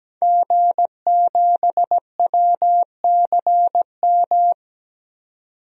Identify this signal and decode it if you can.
Morse code, heard as a keyed tone, about 17 wpm, 700 Hz. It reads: G7WCM
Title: DXCC-100 Call Signs - 5 Characters - 17wpm